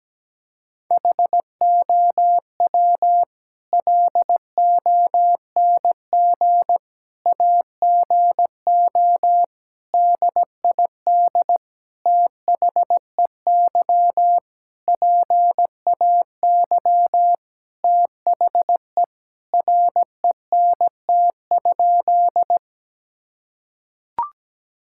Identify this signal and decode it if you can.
Morse code, heard as a keyed tone, about 17 wpm, 700 Hz. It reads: HOW LONG AGO DID THEY PAY THE RENT?
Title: Sentences from Top 1000 Words - Review Part 5 - 17wpm